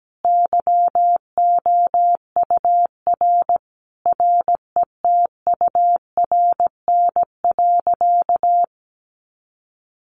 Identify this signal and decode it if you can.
Morse code, heard as a keyed tone, about 17 wpm, 700 Hz. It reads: YOUR RETURN.
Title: A Sound Of Thunder 17wpm